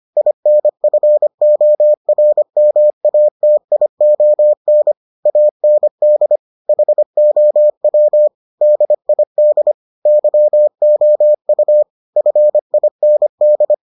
Morse code, heard as a keyed tone, about 25 wpm, 600 Hz. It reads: INFORMATION AND HOW DID YOU FIND